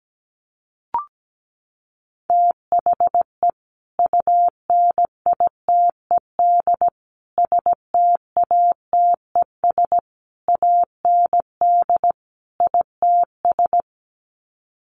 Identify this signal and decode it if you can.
Morse code, heard as a keyed tone, about 17 wpm, 700 Hz. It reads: THE UNITED STATES AND ITS
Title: Sets of 5 Words Encore 17wpm